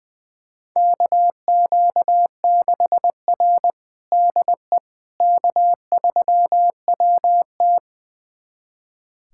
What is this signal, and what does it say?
Morse code, heard as a keyed tone, about 20 wpm, 700 Hz. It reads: KQ6R DE K3WT
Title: USA Callsign DE Callsign - 4 Characters - 20wpm